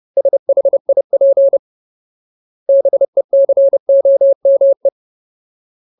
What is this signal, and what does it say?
Morse code, heard as a keyed tone, about 30 wpm, 550 Hz. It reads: SHIP BECOME
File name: Ultimate Word List / 30WPM Ultimate